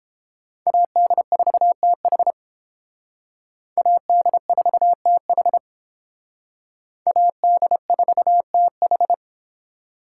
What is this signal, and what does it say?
Morse code, heard as a keyed tone, about 29 wpm, 700 Hz. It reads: AD4TH AD4TH AD4TH